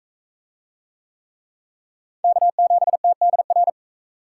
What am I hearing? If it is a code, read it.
Morse code, heard as a keyed tone, about 42 wpm, 700 Hz. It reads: K7TDR